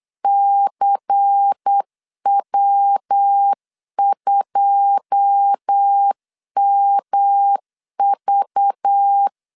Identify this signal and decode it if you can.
Morse code, heard as a keyed tone, about 8 wpm, 800 Hz. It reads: CW2MV
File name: CW / DET / 1 9 1500C S A 8wpm 5637